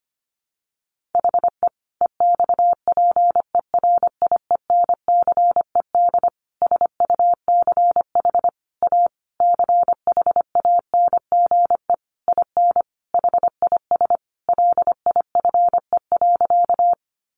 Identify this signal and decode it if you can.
Morse code, heard as a keyed tone, about 25 wpm, 700 Hz. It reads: HE EXPERIENCED SUCH A CHANGE IN HIS LIFE.
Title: Sentences from Top 1100 Words 25wpm